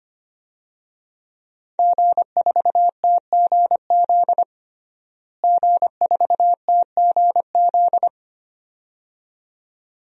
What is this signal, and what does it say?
Morse code, heard as a keyed tone, about 25 wpm, 700 Hz. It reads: G4TGZ G4TGZ
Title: DXCC-100 Call Signs - 5 Characters - Repeated 2x - 25wpm